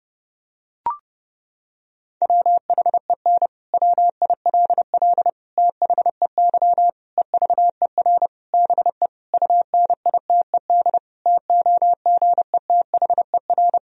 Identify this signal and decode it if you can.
Morse code, heard as a keyed tone, about 30 wpm, 700 Hz. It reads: WHEN WILL THEY EVER BE UNITED TOGETHER
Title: Sentences from Top 1000 Words - Review Part 4 - 30wpm